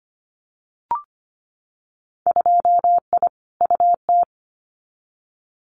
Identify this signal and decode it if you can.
Morse code, heard as a keyed tone, about 25 wpm, 700 Hz. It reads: 2I UT
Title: Winter Field Day Exchanges 25wpm